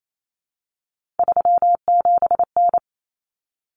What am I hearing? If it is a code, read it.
Morse code, heard as a keyed tone, about 28 wpm, 700 Hz. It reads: 37N
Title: Common Cut Numbers - 3-Digit 28wpm